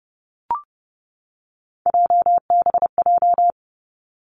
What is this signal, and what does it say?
Morse code, heard as a keyed tone, about 30 wpm, 700 Hz. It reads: JBJ
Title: Three Challenging Letter-Number Combinations 30wpm